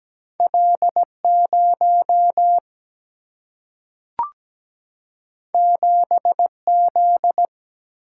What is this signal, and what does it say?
Morse code, heard as a keyed tone, about 17 wpm, 700 Hz. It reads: L 0 7 Z
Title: Single-Letter Single-Number 17wpm